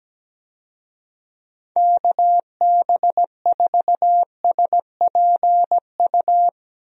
Morse code, heard as a keyed tone, about 17 wpm, 700 Hz. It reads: KB4SPU
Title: USA Call Signs - 6 Characters - Repeated 3x - 17wpm